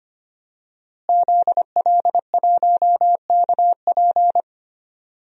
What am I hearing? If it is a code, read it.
Morse code, heard as a keyed tone, about 25 wpm, 700 Hz. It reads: ZL1KP
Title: DXCC-100 Call Signs - 5 Characters - Repeated 3x - 25wpm